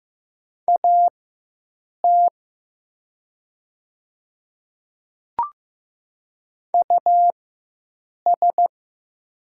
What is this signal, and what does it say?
Morse code, heard as a keyed tone, about 15 wpm, spaced as for 7 wpm, 700 Hz. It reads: AT US
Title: 4x Character Spacing - Top 300 Words - 15wpm